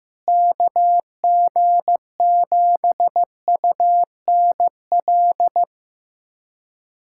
Morse code, heard as a keyed tone, about 15 wpm, 700 Hz. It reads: KG7UNL
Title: USA Callsign DE Callsign - 6 Characters - 15wpm